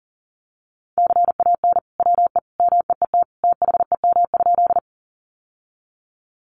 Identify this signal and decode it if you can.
Morse code, heard as a keyed tone, about 37 wpm, 700 Hz. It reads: CAN WE MEET THEM?
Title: Sentences from Top 1100 Words 40wpm